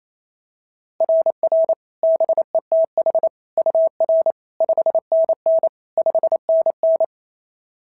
Morse code, heard as a keyed tone, about 25 wpm, 650 Hz. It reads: RR BETH UR 5NN 5NN
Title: SOTA QSO Practice 28wpm